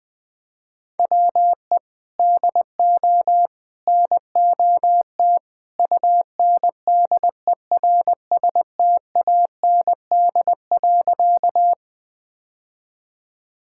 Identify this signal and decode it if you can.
Morse code, heard as a keyed tone, about 20 wpm, 700 Hz. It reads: WE DO NOT UNDERSTAND.
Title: Sentences with 3-Word Phrases - 20wpm